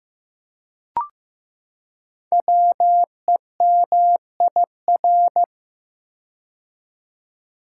Morse code, heard as a keyed tone, about 15 wpm, 700 Hz. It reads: WEMIR
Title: Code Groups - 5 Letters - 15wpm